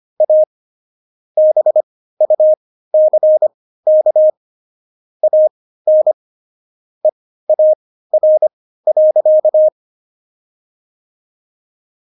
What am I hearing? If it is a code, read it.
Morse code, heard as a keyed tone, about 25 wpm, spaced as for 15 wpm, 620 Hz. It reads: A BUCK AN EAR.
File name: Jokes-25-char-15-eff